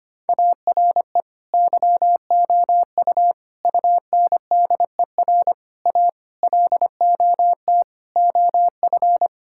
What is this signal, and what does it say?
Morse code, heard as a keyed tone, about 25 wpm, 700 Hz. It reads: ARE YOU UNDER A LOT OF
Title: Sentences from Top 1000 Words - Review Part 6 - 25wpm